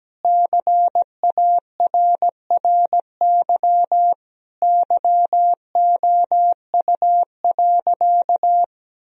Morse code, heard as a keyed tone, about 17 wpm, 700 Hz. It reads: CARRY YOU.
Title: Sentences from Top 1000 Words - Review Part 1 - 17wpm